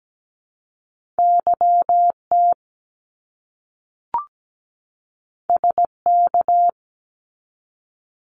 Morse code, heard as a keyed tone, about 17 wpm, 700 Hz. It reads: YT SK
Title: Canadian Provinces and Territories 17wpm